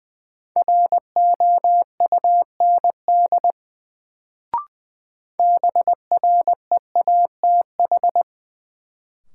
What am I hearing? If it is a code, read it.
Morse code, heard as a keyed tone, about 20 wpm, 700 Hz. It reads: ROUND BREATH
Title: Sets of 5 Words Encore - New Word Drill - 20wpm